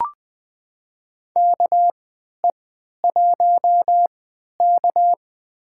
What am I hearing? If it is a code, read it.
Morse code, heard as a keyed tone, about 20 wpm, spaced as for 11 wpm, 700 Hz. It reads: KE1K
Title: USA Call Signs - 3x Character Spacing - 20wpm